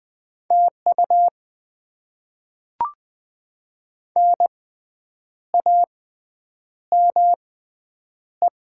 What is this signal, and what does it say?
Morse code, heard as a keyed tone, about 20 wpm, spaced as for 8 wpm, 700 Hz. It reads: TU NAME
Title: Top 20 QSO Elements - 6x Character Spacing - 20wpm